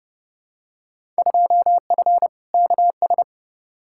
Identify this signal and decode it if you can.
Morse code, heard as a keyed tone, about 30 wpm, 700 Hz. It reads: 2F KS
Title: Field Day Exchanges Encore - 30wpm